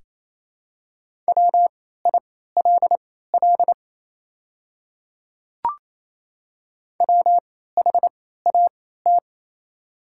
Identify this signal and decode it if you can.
Morse code, heard as a keyed tone, about 27 wpm, spaced as for 16 wpm, 690 Hz. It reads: WILL WHAT
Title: 3x Character Spacing - Top 100 Words - 28wpm